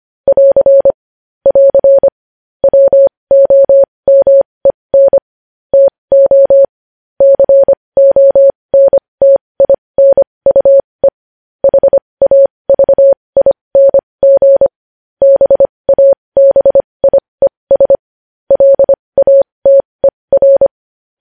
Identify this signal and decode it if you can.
Morse code, heard as a keyed tone, about 25 wpm, spaced as for 20 wpm, 550 Hz. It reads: <AR> <AR> WOMEN TO CONTINUE HAVING BABIES LATER
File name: podcast / News headlines-1x-25x20